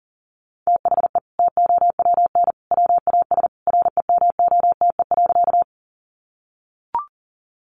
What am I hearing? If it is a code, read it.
Morse code, heard as a keyed tone, about 40 wpm, 700 Hz. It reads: THE TOWN WAS REMOTE.
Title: Sentences from Top 1300 Words 40wpm